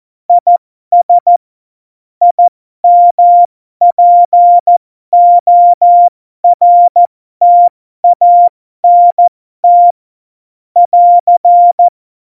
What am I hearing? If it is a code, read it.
Morse code, heard as a keyed tone, about 14 wpm, spaced as for 12 wpm, 700 Hz. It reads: IS IMPORTANT <AR>
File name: Morsekurs / Witze